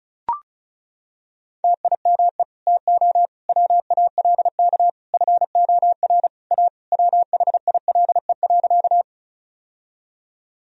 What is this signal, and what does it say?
Morse code, heard as a keyed tone, about 36 wpm, 700 Hz. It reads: TIME TO WALK FOR A WHILE.
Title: Sentences from Top 1000 Words - Review Part 1 - 35wpm